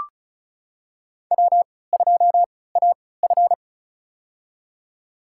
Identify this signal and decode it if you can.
Morse code, heard as a keyed tone, about 34 wpm, spaced as for 20 wpm, 700 Hz. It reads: W2AF
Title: USA Call Signs - 4 Characters - 3x Character Spacing - 35wpm